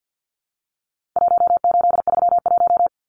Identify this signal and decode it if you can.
Morse code, heard as a keyed tone, about 50 wpm, 700 Hz. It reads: 1831
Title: Numbers - Contest Style - 4-Digit 50wpm